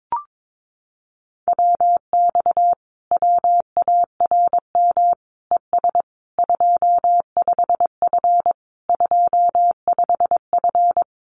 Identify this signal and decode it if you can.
Morse code, heard as a keyed tone, about 22 wpm, 700 Hz. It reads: WX WARM ES 25F 25F
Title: Common QSO Phrases Encore 22wpm